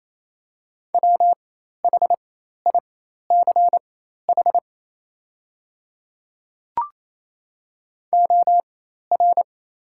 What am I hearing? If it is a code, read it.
Morse code, heard as a keyed tone, about 28 wpm, spaced as for 13 wpm, 700 Hz. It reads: WHICH OR
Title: 4x Character Spacing - Top 100 Words - 28wpm